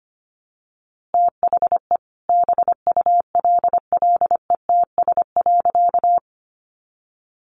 Morse code, heard as a keyed tone, about 25 wpm, 700 Hz. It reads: THE BULLETS.
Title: A Sound Of Thunder 25wpm